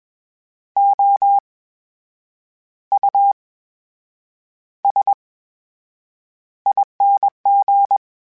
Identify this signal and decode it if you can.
Morse code, heard as a keyed tone, about 21 wpm, spaced as for 7 wpm, 800 Hz. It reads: OUSING
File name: Special INGs-mp3s / Special INGs-mp3s / VerySlowToFaster Ing @20